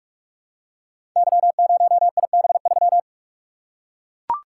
Morse code, heard as a keyed tone, about 44 wpm, 700 Hz. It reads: Y0IB3
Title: Code Groups - 5 Letter:Numbers - 45wpm